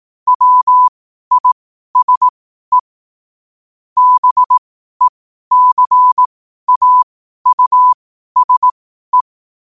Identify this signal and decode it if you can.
Morse code, heard as a keyed tone, about 18 wpm, spaced as for 12 wpm, 1000 Hz. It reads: WISE BECAUSE
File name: qotdfiles / 12wpm